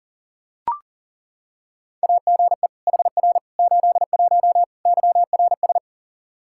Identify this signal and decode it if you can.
Morse code, heard as a keyed tone, about 40 wpm, 700 Hz. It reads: AGE HR 81 YRS
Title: Practice QSO - Exchange 3 - More Info - 40wpm